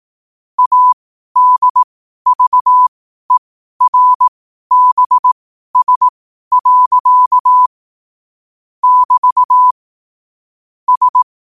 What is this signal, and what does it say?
Morse code, heard as a keyed tone, about 18 wpm, spaced as for 12 wpm, 1000 Hz. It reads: ADVERBS. <BT> S